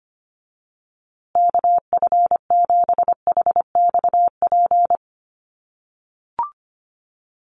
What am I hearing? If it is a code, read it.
Morse code, heard as a keyed tone, about 25 wpm, 700 Hz. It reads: KF7HXP
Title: USA Call Signs Repeated 3x 25wpm